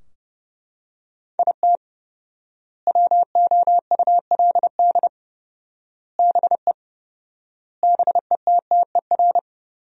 Morse code, heard as a keyed tone, about 30 wpm, 700 Hz. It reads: IT WOULD BE BETTER
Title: 4x Word-Spacing - Sentences from Top 400 Words - 30wpm